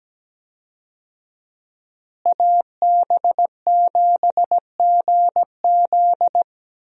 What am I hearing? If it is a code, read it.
Morse code, heard as a keyed tone, about 17 wpm, 700 Hz. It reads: AB7GZ
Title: USA Call Signs - 5 Characters - Repeated 2x - 17wpm